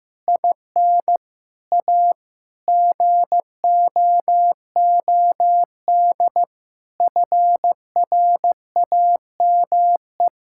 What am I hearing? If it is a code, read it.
Morse code, heard as a keyed tone, about 16 wpm, 700 Hz. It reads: IN A GOOD FRAME
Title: Sentences from Top 1000 Words - Review Part 6 - 15wpm